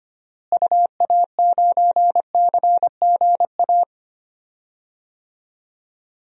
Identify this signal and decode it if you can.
Morse code, heard as a keyed tone, about 25 wpm, 700 Hz. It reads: UA9CGA
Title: DXCC-100 Call Signs - 6 Characters - Repeated 3x - 25wpm